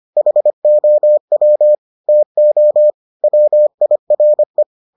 Morse code, heard as a keyed tone, about 25 wpm, 600 Hz. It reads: HOW TO WIRE